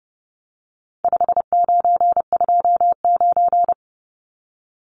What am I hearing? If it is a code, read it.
Morse code, heard as a keyed tone, about 30 wpm, 700 Hz. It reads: 5929
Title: Numbers - 4-Digit 30wpm